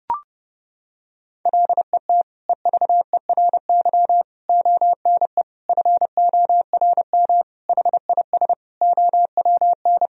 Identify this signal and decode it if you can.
Morse code, heard as a keyed tone, about 32 wpm, 700 Hz. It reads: LET EVERY ONE FORM HIS OWN
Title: Sentences from Top 1000 Words - Review Part 1 - 30wpm